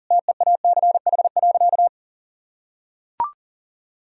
Morse code, heard as a keyed tone, about 40 wpm, 700 Hz. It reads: TEACH.
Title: Sentences from Top 1000 Words - Review Part 2 - 40wpm